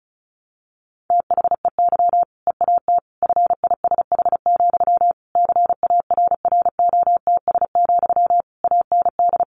Morse code, heard as a keyed tone, about 34 wpm, 700 Hz. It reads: THEY EAT FISH, CARROTS, AND
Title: Easy Phrases 35wpm